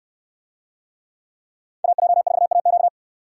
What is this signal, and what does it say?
Morse code, heard as a keyed tone, about 69 wpm, 700 Hz. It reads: AJ4AQ